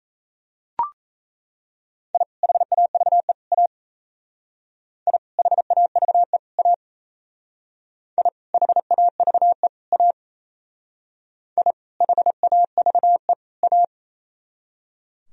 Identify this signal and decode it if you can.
Morse code, heard as a keyed tone, about 35 wpm, 700 Hz. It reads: I HAVE A I HAVE A I HAVE A I HAVE A
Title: Speed-Racing - Sets of 3 Words 28wpm